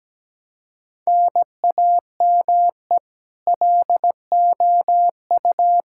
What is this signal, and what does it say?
Morse code, heard as a keyed tone, about 17 wpm, 700 Hz. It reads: NAME LOU
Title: Practice QSO - Exchange 1 - Beginning - 17wpm